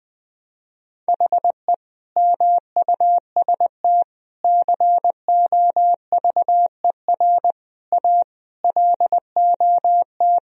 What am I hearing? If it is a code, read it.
Morse code, heard as a keyed tone, about 20 wpm, 700 Hz. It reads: HE MUST COVER A LOT
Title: Sentences from Top 500 Words and Abbreviations 20wpm